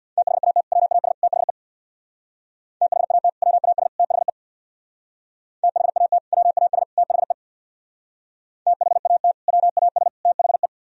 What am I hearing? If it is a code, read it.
Morse code, heard as a keyed tone, about 62 wpm, 700 Hz. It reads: THAT WAS THE THAT WAS THE THAT WAS THE THAT WAS THE
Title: Speed-Racing - Sets of 3 Words 50wpm